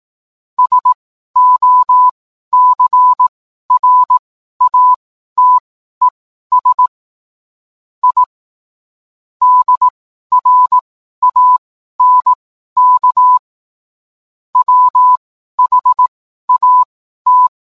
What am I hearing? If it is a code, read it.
Morse code, heard as a keyed tone, about 18 wpm, spaced as for 12 wpm, 1000 Hz. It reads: SOCRATES I DRANK WHAT